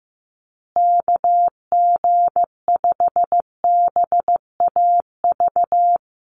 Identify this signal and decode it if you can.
Morse code, heard as a keyed tone, about 15 wpm, 700 Hz. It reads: KG5BAV
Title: USA Call Signs Repeated 2x 15wpm